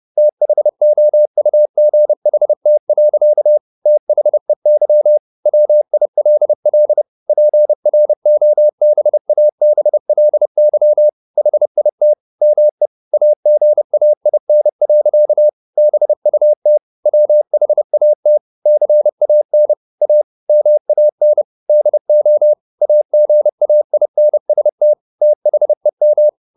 Morse code, heard as a keyed tone, about 30 wpm, 600 Hz. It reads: THOUGHT. THEY WILL PROBABLY HIT ME AGAIN. BUT WHAT CAN A MAN DO AGAINST THEM